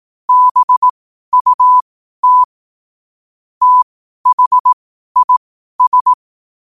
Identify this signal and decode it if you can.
Morse code, heard as a keyed tone, about 18 wpm, spaced as for 12 wpm, 1000 Hz. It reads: BUT THIS
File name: qotdfiles / 12wpm